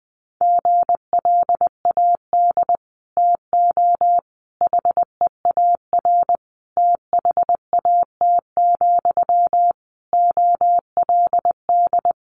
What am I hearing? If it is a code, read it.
Morse code, heard as a keyed tone, about 20 wpm, 700 Hz. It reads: GLAD TO HEAR THAT, OLD
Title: Sentences from Top 800 Words 20wpm